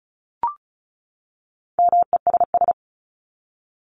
Morse code, heard as a keyed tone, about 34 wpm, 700 Hz. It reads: MESS
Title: Top 1751-2000 Words 35wpm